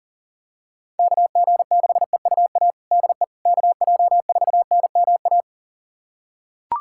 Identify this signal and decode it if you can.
Morse code, heard as a keyed tone, about 40 wpm, 700 Hz. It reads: KG6EUA DE KJ4NMA
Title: USA Callsign DE Callsign - 6 Characters - 40wpm